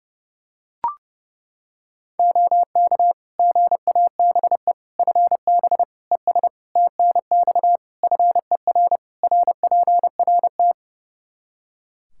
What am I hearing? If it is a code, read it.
Morse code, heard as a keyed tone, about 30 wpm, 700 Hz. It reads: OK GABE FB ES TNX FER RPRT
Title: Practice QSO - Exchange 2 - Info - 30wpm